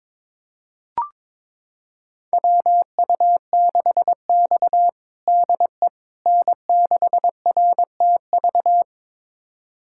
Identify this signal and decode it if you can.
Morse code, heard as a keyed tone, about 22 wpm, 700 Hz. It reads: WU6X DE N6RTV
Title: USA Callsign DE Callsign 22wpm